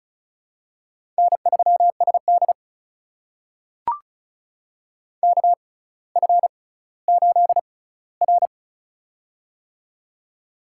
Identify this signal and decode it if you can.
Morse code, heard as a keyed tone, about 36 wpm, spaced as for 18 wpm, 700 Hz. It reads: N3SD KF8R
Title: USA Call Signs - 4 Characters - 6x Character Spacing - 35wpm